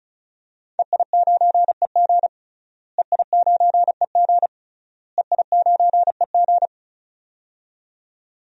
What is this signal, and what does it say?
Morse code, heard as a keyed tone, about 36 wpm, 700 Hz. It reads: EI9EG EI9EG EI9EG
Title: DXCC-100 Call Signs - 5 Characters - Repeated 3x - 35wpm